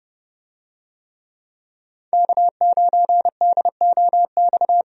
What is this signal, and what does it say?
Morse code, heard as a keyed tone, about 30 wpm, 700 Hz. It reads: K9DOX
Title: USA Call Signs - 5 Characters - 30wpm